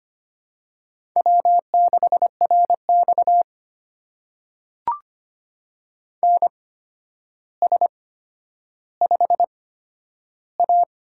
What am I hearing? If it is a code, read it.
Morse code, heard as a keyed tone, about 25 wpm, spaced as for 11 wpm, 700 Hz. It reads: W6RX NS5A
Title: USA Call Signs - 4 Characters - 8x Character Spacing - 25wpm